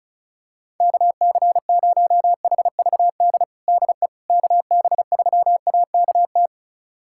Morse code, heard as a keyed tone, about 34 wpm, 700 Hz. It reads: KC0HVD DE KB3AKT
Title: USA Callsign DE Callsign 35wpm